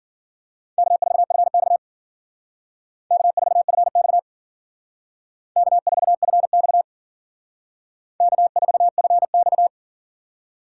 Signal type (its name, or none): Morse code